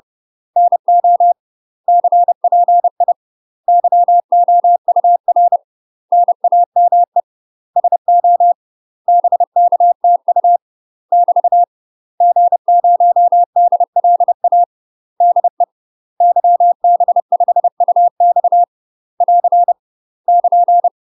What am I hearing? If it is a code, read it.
Morse code, heard as a keyed tone, about 30 wpm, 700 Hz. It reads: NO CPI YOUR NAME SO BKTU <BT> G0DLA DE YB5UX <AR> <KN>
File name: UK MASTER QSOs TRAINING / 30WPM / UK MASTER QSOs 0023